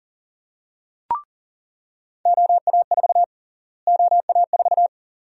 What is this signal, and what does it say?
Morse code, heard as a keyed tone, about 39 wpm, 700 Hz. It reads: OA4 OA4
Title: USA Call Sign Prefixes - Repeated 2x - 40wpm